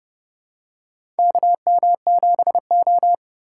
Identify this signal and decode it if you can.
Morse code, heard as a keyed tone, about 32 wpm, 700 Hz. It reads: KM7O
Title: USA Call Signs - 4 Characters - 2x Character Spacing - 30wpm